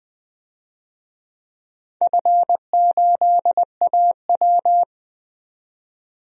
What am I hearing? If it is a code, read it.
Morse code, heard as a keyed tone, about 20 wpm, 700 Hz. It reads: F8AW
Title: DXCC-100 Call Signs - 4 Characters - 20wpm